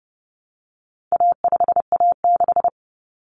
Morse code, heard as a keyed tone, about 30 wpm, 700 Hz. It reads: A5A6
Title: Common Cut Numbers - 4-Digit 30wpm